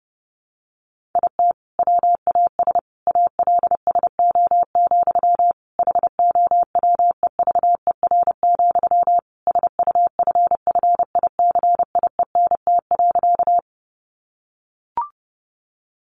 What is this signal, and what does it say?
Morse code, heard as a keyed tone, about 30 wpm, 700 Hz. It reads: IT WAS ALSO, HOWEVER, SUFFICIENT.
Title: Sentences from Top 1400 Words 30wpm